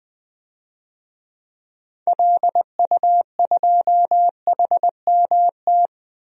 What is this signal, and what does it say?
Morse code, heard as a keyed tone, about 20 wpm, 700 Hz. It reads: LU2HMT
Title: DXCC-100 Call Signs - 6 Characters - 20wpm